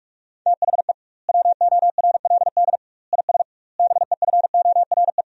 Morse code, heard as a keyed tone, about 41 wpm, 700 Hz. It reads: THE WORLD IS BEFORE
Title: 2x Word-Spacing - Sentences from Top 200 Words - 45wpm